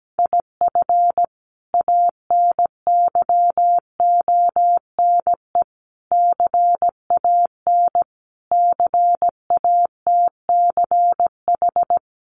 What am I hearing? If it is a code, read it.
Morse code, heard as a keyed tone, about 17 wpm, 700 Hz. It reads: IF ANYONE CAN CATCH
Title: Sentences from Top 1500 Words 17wpm